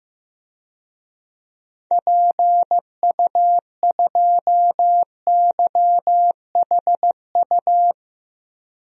Morse code, heard as a keyed tone, about 16 wpm, 700 Hz. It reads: PU2YHU
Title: DXCC-100 Call Signs - 6 Characters - Repeated 2x - 15wpm